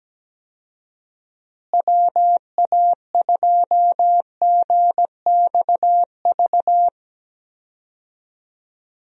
Morse code, heard as a keyed tone, about 17 wpm, 700 Hz. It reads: WA2GXV
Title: USA Call Signs - 6 Characters - Repeated 2x - 17wpm